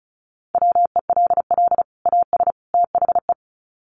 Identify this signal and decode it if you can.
Morse code, heard as a keyed tone, about 34 wpm, 700 Hz. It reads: WELL AS THE
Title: Sets of 3 Words 35wpm